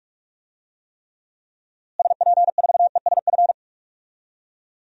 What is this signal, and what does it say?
Morse code, heard as a keyed tone, about 46 wpm, 700 Hz. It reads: SP4ESF
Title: DXCC-100 Call Signs - 6 Characters - Repeated 2x - 45wpm